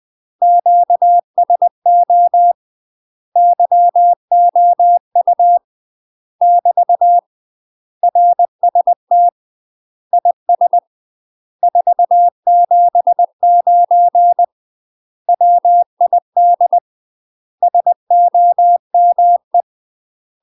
Morse code, heard as a keyed tone, about 20 wpm, 700 Hz. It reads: QSO YOU <BT> RST IS 479 WID SOME
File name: UK MASTER QSOs TRAINING / 20WPM / UK MASTER QSOs 0014